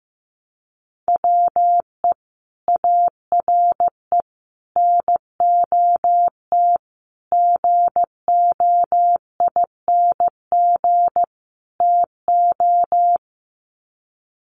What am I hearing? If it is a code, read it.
Morse code, heard as a keyed tone, about 15 wpm, 700 Hz. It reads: WE ARE NOT GOING TO
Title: Sets of 5 Words 15wpm